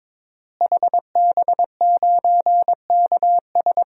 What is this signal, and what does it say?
Morse code, heard as a keyed tone, about 22 wpm, 700 Hz. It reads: HB9KS
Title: DXCC-100 Call Signs - 5 Characters - Repeated 2x - 22wpm